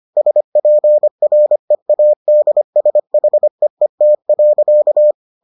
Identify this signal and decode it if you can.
Morse code, heard as a keyed tone, about 25 wpm, 600 Hz. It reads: SPREADSHEET.